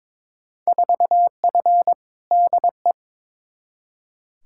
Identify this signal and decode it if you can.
Morse code, heard as a keyed tone, about 22 wpm, 700 Hz. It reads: 4F DE
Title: Field Day Exchanges Encore - 22wpm